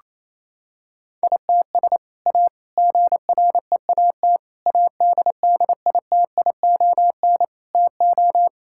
Morse code, heard as a keyed tone, about 28 wpm, 700 Hz. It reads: ITS A GREAT ADDITION TO
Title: Sentences from Top 1000 Words - Review Part 6 - 28wpm